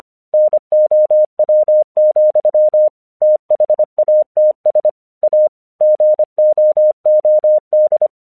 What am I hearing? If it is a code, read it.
Morse code, heard as a keyed tone, about 25 wpm, 610 Hz. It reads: NOW, THATS A GOOD
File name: 1 English 25wpm / AQC28